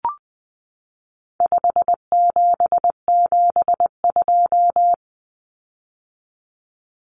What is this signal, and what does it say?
Morse code, heard as a keyed tone, about 20 wpm, 700 Hz. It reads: 5772
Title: Common Cut Numbers - 4-Digit 20wpm